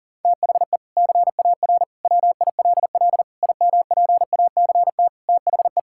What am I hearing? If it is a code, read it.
Morse code, heard as a keyed tone, about 40 wpm, 700 Hz. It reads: THE CAR WILL IMPACT THE
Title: Sentences from Top 1000 Words - Review Part 6 - 40wpm